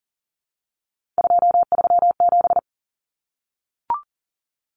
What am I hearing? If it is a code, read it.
Morse code, heard as a keyed tone, about 40 wpm, 700 Hz. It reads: 237
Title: Numbers - 3-Digit 40wpm